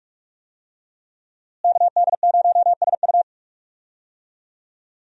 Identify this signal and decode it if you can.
Morse code, heard as a keyed tone, about 45 wpm, 700 Hz. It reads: KD0SU
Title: USA Call Signs - 5 Characters - Repeated 2x - 45wpm